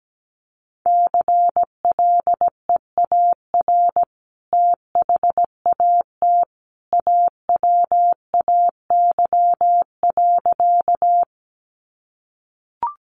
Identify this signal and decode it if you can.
Morse code, heard as a keyed tone, about 17 wpm, 700 Hz. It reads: CLEAR THAT AWAY.
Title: Sentences from Top 600 Words 17wpm